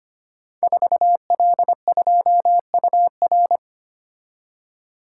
Morse code, heard as a keyed tone, about 25 wpm, 700 Hz. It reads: 4L2UR
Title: Code Groups - 5 Letter:Numbers - 25wpm